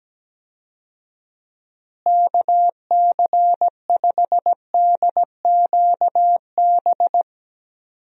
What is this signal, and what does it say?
Morse code, heard as a keyed tone, about 17 wpm, 700 Hz. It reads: KC5DQB
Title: USA Call Signs - 6 Characters - Repeated 3x - 17wpm